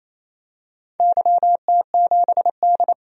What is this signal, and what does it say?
Morse code, heard as a keyed tone, about 28 wpm, 700 Hz. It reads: YT7D
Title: DXCC-100 Call Signs - 4 Characters - 28wpm